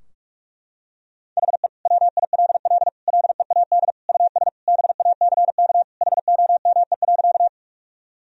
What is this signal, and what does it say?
Morse code, heard as a keyed tone, about 41 wpm, 700 Hz. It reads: HE WILL LEAD US BACK HOME.
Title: Sentences from Top 1000 Words - Review Part 2 - 45wpm